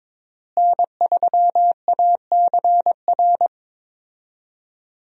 Morse code, heard as a keyed tone, about 21 wpm, 700 Hz. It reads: N3ACR
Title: USA Call Signs - 5 Characters - Repeated 3x - 22wpm